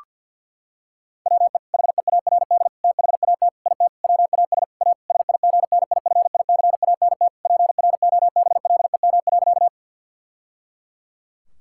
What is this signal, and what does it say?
Morse code, heard as a keyed tone, about 52 wpm, 700 Hz. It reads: WE HEARD THAT IT WAS A SIGNIFICANT PROBLEM.
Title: Sentences from Top 1000 Words - Review Part 6 - 50wpm